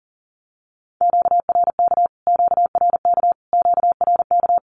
Morse code, heard as a keyed tone, about 40 wpm, 700 Hz. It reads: QRK QRK QRK
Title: Q-Codes Repeated 3x 40wpm